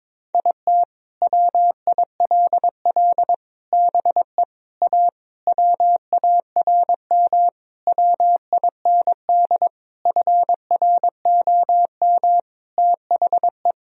Morse code, heard as a keyed tone, about 22 wpm, 700 Hz. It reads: IT WILL BE A WARM WIND FROM THE
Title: Sentences from Top 1000 Words - Review Part 3 - 22wpm